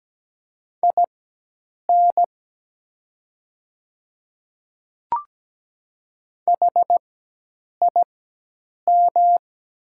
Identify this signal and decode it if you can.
Morse code, heard as a keyed tone, about 17 wpm, spaced as for 8 wpm, 700 Hz. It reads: IN HIM
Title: 4x Character Spacing - Top 100 Words - 17wpm